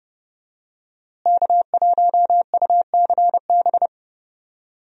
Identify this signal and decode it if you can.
Morse code, heard as a keyed tone, about 30 wpm, 700 Hz. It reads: K1UCB
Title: USA Call Signs - 5 Characters - Repeated 2x - 30wpm